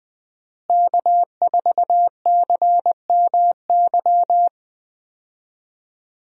Morse code, heard as a keyed tone, about 20 wpm, 700 Hz. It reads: K4CMY
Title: USA Call Signs - 5 Characters - Repeated 2x - 20wpm